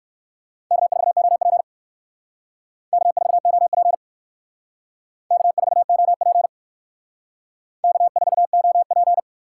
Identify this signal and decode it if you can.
Morse code, heard as a keyed tone, about 65 wpm, 700 Hz. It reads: K4OP K4OP K4OP K4OP